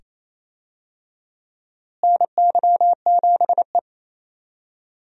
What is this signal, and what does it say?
Morse code, heard as a keyed tone, about 28 wpm, 700 Hz. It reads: NY7E
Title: USA Call Signs - 4 Characters - Repeated 2x - 28wpm